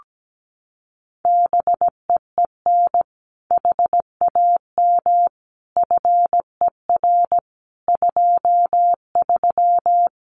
Morse code, heard as a keyed tone, about 17 wpm, 700 Hz. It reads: BEEN HAM FER 23
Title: Common QSO Phrases Encore 17wpm